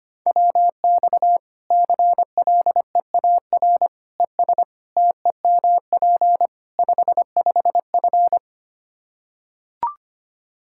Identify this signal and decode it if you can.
Morse code, heard as a keyed tone, about 25 wpm, 700 Hz. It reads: WX CLEAR ES TEMP 55F
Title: Practice QSO - Exchange 2 - Info - 25wpm